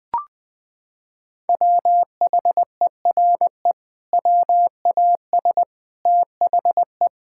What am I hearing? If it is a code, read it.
Morse code, heard as a keyed tone, about 20 wpm, 700 Hz. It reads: WHERE WAS THE
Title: Sentences from Top 1600 Words 20wpm